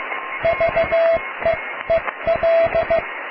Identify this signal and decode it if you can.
Morse code, heard as a keyed tone, about 15 wpm, 660 Hz, with radio noise. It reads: VEEL
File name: cw / tekst(a)+QRM